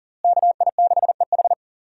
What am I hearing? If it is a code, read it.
Morse code, heard as a keyed tone, about 40 wpm, 700 Hz. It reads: KI6EH
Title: USA Call Signs - 5 Characters - Repeated 3x - 40wpm